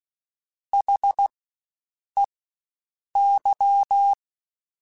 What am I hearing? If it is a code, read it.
Morse code, heard as a keyed tone, about 16 wpm, spaced as for 7 wpm, 800 Hz. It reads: HEY
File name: Spaced Alphas / Spaced Alphas mp3s / Hs sp @15